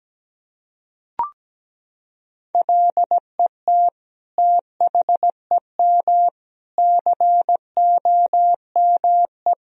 Morse code, heard as a keyed tone, about 17 wpm, 700 Hz. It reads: LET THEM COME
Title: Sentences from Top 1000 Words - Review Part 3 - 17wpm